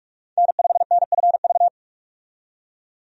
Morse code, heard as a keyed tone, about 47 wpm, 700 Hz. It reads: N5NFV
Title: USA Call Signs - 5 Characters - 45wpm